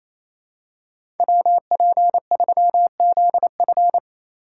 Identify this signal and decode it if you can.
Morse code, heard as a keyed tone, about 28 wpm, 700 Hz. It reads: WP3ZF
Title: USA Call Signs - 5 Characters - 2x Character Spacing - 28wpm